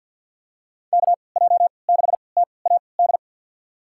Morse code, heard as a keyed tone, about 46 wpm, spaced as for 27 wpm, 690 Hz. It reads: KJ6TAD
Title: USA Call Signs - 3x Character Spacing - 50wpm